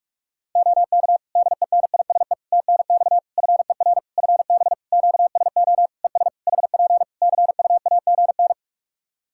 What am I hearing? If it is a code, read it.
Morse code, heard as a keyed tone, about 41 wpm, 700 Hz. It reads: OK DENISE TNX FER FB QSO ES HP CUAGN
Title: Practice QSO - Exchange 4 - Ending - 45wpm